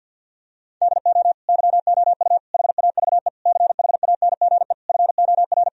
Morse code, heard as a keyed tone, about 49 wpm, 700 Hz. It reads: DO YOU HAVE CHANGE FOR
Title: Sentences from Top 1000 Words - Review Part 2 - 50wpm